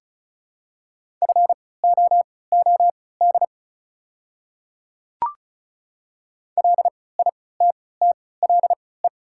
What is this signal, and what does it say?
Morse code, heard as a keyed tone, about 34 wpm, spaced as for 20 wpm, 700 Hz. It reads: FOOD LITTLE
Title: 3x Character Spacing - Top 300 Words - 35wpm